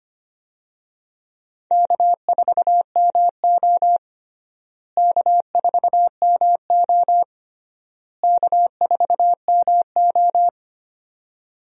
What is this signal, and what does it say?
Morse code, heard as a keyed tone, about 25 wpm, 700 Hz. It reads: K4MO K4MO K4MO
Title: USA Call Signs - 4 Characters - Repeated 3x - 25wpm